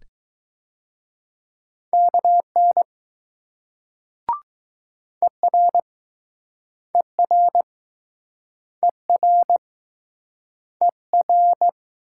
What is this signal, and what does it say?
Morse code, heard as a keyed tone, about 20 wpm, 700 Hz. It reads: KN ER ER ER ER